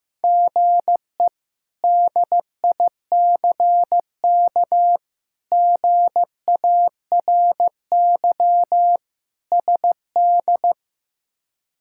Morse code, heard as a keyed tone, about 15 wpm, 700 Hz. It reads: GE DICK GARY SD
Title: K1USN SST QSO Practice - 15wpm